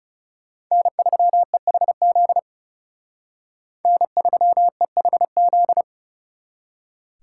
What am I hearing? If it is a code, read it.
Morse code, heard as a keyed tone, about 39 wpm, 700 Hz. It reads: N3EHZ N3EHZ